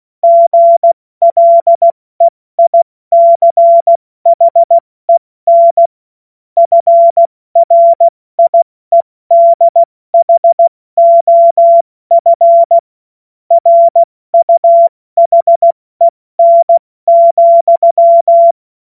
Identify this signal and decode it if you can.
Morse code, heard as a keyed tone, about 16 wpm, 680 Hz. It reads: GLEICHEN FRIEDHOF RUHEN,
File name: Morsekurs / Witze / Witz 04. Januar 2022, 70 F80